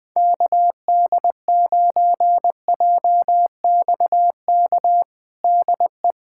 Morse code, heard as a keyed tone, about 20 wpm, 700 Hz. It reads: KD9JXK DE
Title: USA Callsign DE Callsign - 6 Characters - 20wpm